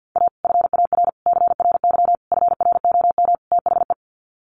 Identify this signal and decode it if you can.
Morse code, heard as a keyed tone, about 50 wpm, 700 Hz. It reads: A FAR CRY FROM THE
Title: Sets of 5 Words Encore 50wpm